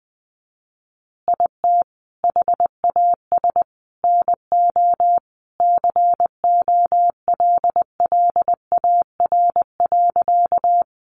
Morse code, heard as a keyed tone, about 20 wpm, 700 Hz. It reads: IT HAS NO COLLAR.
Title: Easy Phrases 20wpm